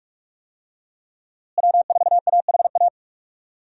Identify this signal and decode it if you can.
Morse code, heard as a keyed tone, about 46 wpm, 700 Hz. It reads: W4AHA
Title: USA Call Signs - 5 Characters - Repeated 3x - 45wpm